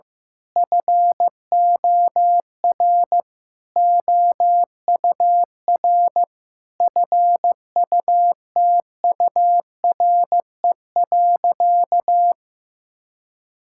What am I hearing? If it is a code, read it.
Morse code, heard as a keyed tone, about 15 wpm, 700 Hz. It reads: FOR OUR FUTURE.
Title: Sentences from Top 1000 Words - Review Part 6 - 15wpm